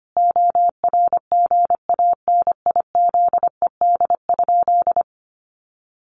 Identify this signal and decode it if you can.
Morse code, heard as a keyed tone, about 25 wpm, 700 Hz. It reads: ORGANIZED?
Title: Sentences from Top 1300 Words 25wpm